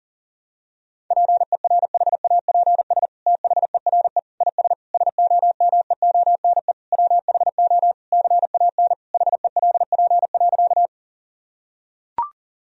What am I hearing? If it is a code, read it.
Morse code, heard as a keyed tone, about 37 wpm, 700 Hz. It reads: PERHAPS THERE IS SOMEONE WHO CAN HELP.
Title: Sentences from Top 1000 Words - Review Part 4 - 40wpm